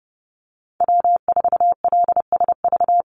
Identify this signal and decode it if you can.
Morse code, heard as a keyed tone, about 30 wpm, 700 Hz. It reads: W4LSV
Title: Call Signs Repeated 2x - 30wpm